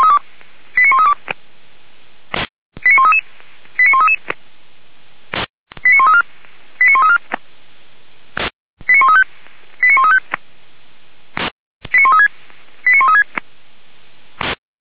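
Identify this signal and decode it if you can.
Five-tone selective call (ZVEI), heard as quick runs of five tones, tones of about 70 ms. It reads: 89132 89133 89133 89134 89134 89135 89135 89136 89136
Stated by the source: ZVEI five-tone call 89131 received by SDR.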